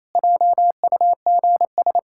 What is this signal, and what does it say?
Morse code, heard as a keyed tone, about 28 wpm, 700 Hz. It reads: JUGS
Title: Pangram Challenge v2 - 28wpm